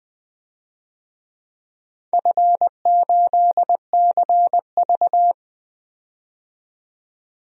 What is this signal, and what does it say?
Morse code, heard as a keyed tone, about 20 wpm, 700 Hz. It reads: F8CV
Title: DXCC-100 Call Signs - 4 Characters - 20wpm